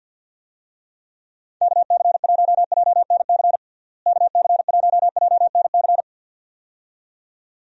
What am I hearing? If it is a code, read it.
Morse code, heard as a keyed tone, about 47 wpm, 700 Hz. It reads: KC1JNC KC1JNC
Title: USA Call Signs - 6 Characters - Repeated 2x - 50wpm